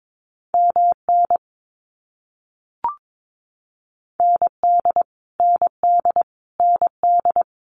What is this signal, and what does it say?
Morse code, heard as a keyed tone, about 22 wpm, 700 Hz. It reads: MN ND ND ND
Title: US States Repeated 3x 22wpm